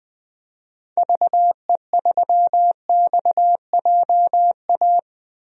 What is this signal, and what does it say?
Morse code, heard as a keyed tone, about 20 wpm, 700 Hz. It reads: VE3XJA
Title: DXCC-100 Call Signs - 6 Characters - Repeated 2x - 20wpm